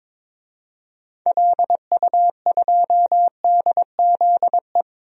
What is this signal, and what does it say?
Morse code, heard as a keyed tone, about 22 wpm, 700 Hz. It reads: LU2DZE
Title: DXCC-100 Call Signs - 6 Characters - 22wpm